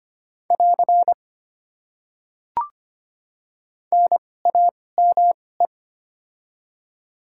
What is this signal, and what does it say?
Morse code, heard as a keyed tone, about 25 wpm, spaced as for 18 wpm, 700 Hz. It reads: <AR> NAME
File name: Top 90 QSO Elements - 2x Character Spacing - 25wpm